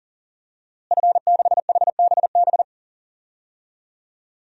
Morse code, heard as a keyed tone, about 37 wpm, 700 Hz. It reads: F6HBB
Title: DXCC-100 Call Signs - 5 Characters - Repeated 2x - 40wpm